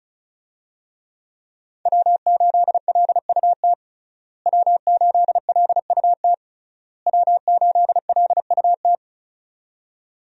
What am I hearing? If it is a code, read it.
Morse code, heard as a keyed tone, about 36 wpm, 700 Hz. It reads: W8LUT W8LUT W8LUT
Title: USA Call Signs - 5 Characters - Repeated 3x - 35wpm